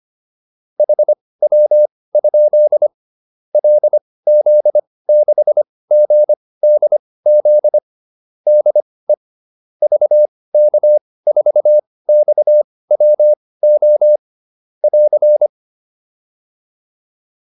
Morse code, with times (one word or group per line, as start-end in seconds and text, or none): 0.8-2.9 HW?
3.5-7.8 LZ6GDZ
8.5-9.1 DE
9.8-14.2 VK4XWO
14.8-15.5 <AR>